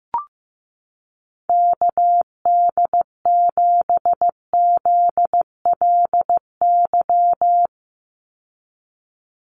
Morse code, heard as a keyed tone, about 15 wpm, 700 Hz. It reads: KD7ZLY
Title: USA Call Signs 15wpm